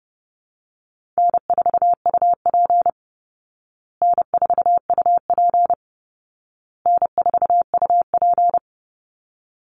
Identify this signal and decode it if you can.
Morse code, heard as a keyed tone, about 30 wpm, 700 Hz. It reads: N4UP N4UP N4UP
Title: Easy Call Signs Repeated 3x - 30wpm